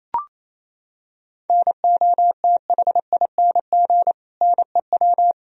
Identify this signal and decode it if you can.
Morse code, heard as a keyed tone, about 28 wpm, 700 Hz. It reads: NOTHING NEW
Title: Sentences from Top 1000 Words - Review Part 1 - 28wpm